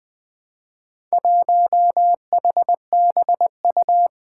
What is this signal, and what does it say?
Morse code, heard as a keyed tone, about 20 wpm, 700 Hz. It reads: 1HBU
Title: Four Challenging Letter-Number Combinations 20wpm